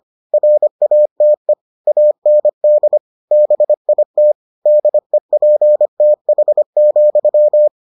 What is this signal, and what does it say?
Morse code, heard as a keyed tone, about 25 wpm, 600 Hz. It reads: RATE AND BIT DEPTH,